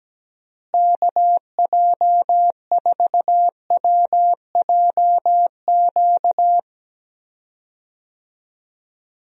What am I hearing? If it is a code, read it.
Morse code, heard as a keyed tone, about 17 wpm, 700 Hz. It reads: KJ4WJQ
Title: USA Call Signs - 6 Characters - Repeated 2x - 17wpm